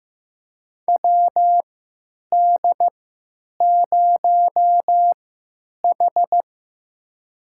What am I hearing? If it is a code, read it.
Morse code, heard as a keyed tone, about 15 wpm, spaced as for 9 wpm, 700 Hz. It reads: WD0H
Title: USA Call Signs - 4 Characters - 3x Character Spacing - 15wpm